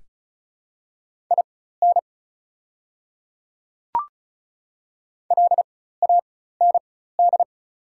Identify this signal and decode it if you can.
Morse code, heard as a keyed tone, about 36 wpm, spaced as for 16 wpm, 690 Hz. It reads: IN LAND
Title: 4x Character Spacing - Top 200 Words - 35wpm